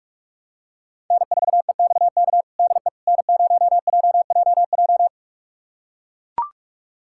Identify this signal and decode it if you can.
Morse code, heard as a keyed tone, about 42 wpm, 700 Hz. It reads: N4EXK DE N0JJJ
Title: USA Callsign DE Callsign - 5 Characters - 45wpm